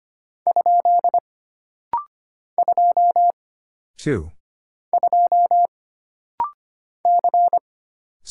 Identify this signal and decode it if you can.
Morse code, heard as a keyed tone, about 25 wpm, 700 Hz. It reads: ? 2 2 C
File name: Single Character (Letter:Number:Punctuation:Prosign) - Rapid-Fire - 25wpm